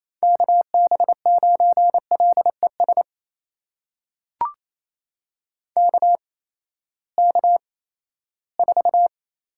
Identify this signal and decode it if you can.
Morse code, heard as a keyed tone, about 28 wpm, spaced as for 16 wpm, 700 Hz. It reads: KB9LES KK4
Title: USA Call Signs - 6 Characters - 8x Character Spacing - 28wpm